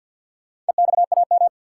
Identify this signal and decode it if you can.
Morse code, heard as a keyed tone, about 52 wpm, 700 Hz. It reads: EXAM
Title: Four Letter Words - 50wpm